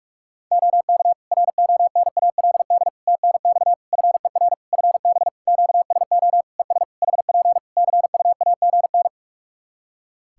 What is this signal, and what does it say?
Morse code, heard as a keyed tone, about 46 wpm, 700 Hz. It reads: OK RONALD TNX FER FB QSO ES HP CUAGN
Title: Practice QSO - Exchange 4 - Ending - 45wpm